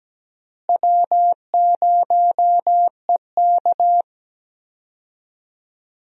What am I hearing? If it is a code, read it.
Morse code, heard as a keyed tone, about 17 wpm, 700 Hz. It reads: W0EK
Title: USA Call Signs - 4 Characters - Repeated 3x - 17wpm